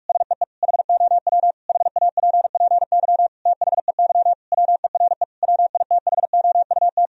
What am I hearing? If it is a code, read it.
Morse code, heard as a keyed tone, about 46 wpm, 700 Hz. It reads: SEE HOW HAPPY THEY WERE WITHOUT
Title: Sentences from Top 1000 Words - Review Part 3 - 45wpm